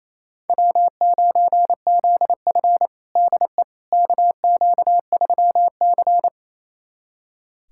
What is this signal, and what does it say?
Morse code, heard as a keyed tone, about 28 wpm, 700 Hz. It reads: W9ZF DE KQ3C
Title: USA Callsign DE Callsign - 4 Characters - 28wpm